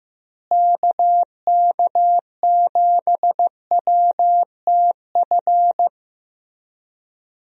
Morse code, heard as a keyed tone, about 15 wpm, 700 Hz. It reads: KK7WTF
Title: USA Call Signs - 6 Characters - Repeated 2x - 15wpm